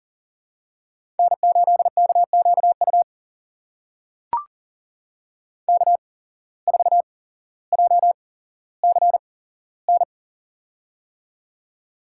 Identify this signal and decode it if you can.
Morse code, heard as a keyed tone, about 40 wpm, spaced as for 17 wpm, 700 Hz. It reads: N8KQU K4JCN
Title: USA Call Signs - 5 Characters - 8x Character Spacing - 40wpm